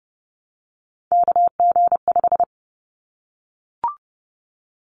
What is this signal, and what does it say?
Morse code, heard as a keyed tone, about 30 wpm, 700 Hz. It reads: KG5
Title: Callsign Prefixes for Top 25 DXCC - 30wpm